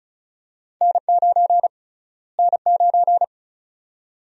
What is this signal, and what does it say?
Morse code, heard as a keyed tone, about 34 wpm, 700 Hz. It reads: N9 N9
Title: USA Call Sign Prefixes - Repeated 2x - 35wpm